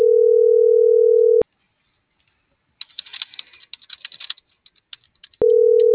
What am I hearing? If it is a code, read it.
Ringback tone (North America), 440 + 480 Hz.